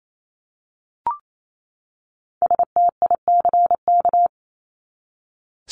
Morse code, heard as a keyed tone, about 28 wpm, 700 Hz. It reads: STICK
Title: Top 1001-1250 Words 28wpm